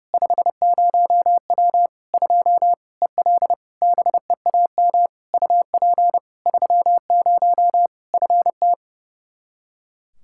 Morse code, heard as a keyed tone, about 30 wpm, 700 Hz. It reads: 50W 2 EL BEAM UP 30 FT
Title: Practice QSO - Exchange 2 - Info - 30wpm